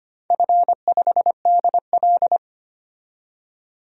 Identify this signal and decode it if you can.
Morse code, heard as a keyed tone, about 25 wpm, 700 Hz. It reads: F5DL
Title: DXCC-100 Call Signs - 4 Characters - 25wpm